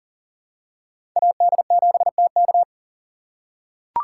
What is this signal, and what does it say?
Morse code, heard as a keyed tone, about 40 wpm, 700 Hz. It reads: AD7TK
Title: USA Call Signs - 5 Characters - 2x Character Spacing - 40wpm